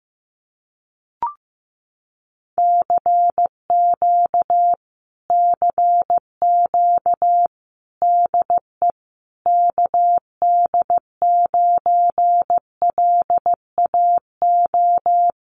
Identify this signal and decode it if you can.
Morse code, heard as a keyed tone, about 15 wpm, 700 Hz. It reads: CQ CQ DE KD9LAO
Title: Common QSO Phrases Encore 15wpm